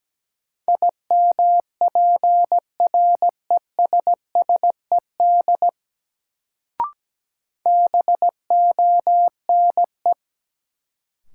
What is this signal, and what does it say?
Morse code, heard as a keyed tone, about 17 wpm, 700 Hz. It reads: IMPRESSED BONE
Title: Sentences from Top 1600 Words - New Word Drill Alt - 17wpm